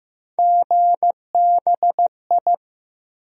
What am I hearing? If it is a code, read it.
Morse code, heard as a keyed tone, about 15 wpm, 700 Hz. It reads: GBI
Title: USA Call Sign Postfixes - 3 Letters 15wpm